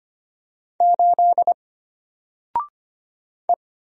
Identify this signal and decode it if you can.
Morse code, heard as a keyed tone, about 25 wpm, 700 Hz. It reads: 8 E
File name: Single-Letter-Number-25wpm